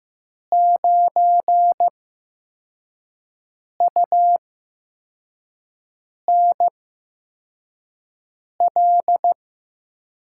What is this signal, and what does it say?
Morse code, heard as a keyed tone, about 15 wpm, spaced as for 4 wpm, 700 Hz. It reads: 9UNL
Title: USA Call Signs - 6 Characters - 8x Character Spacing - 15wpm